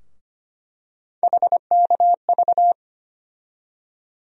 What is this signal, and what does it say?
Morse code, heard as a keyed tone, about 25 wpm, 700 Hz. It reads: HKV
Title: USA Call Sign Postfixes - 3 Letters 25wpm